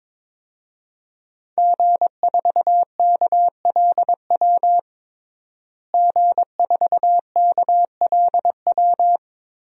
Morse code, heard as a keyed tone, about 21 wpm, 700 Hz. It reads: G4KLW G4KLW
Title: DXCC-100 Call Signs - 5 Characters - Repeated 3x - 22wpm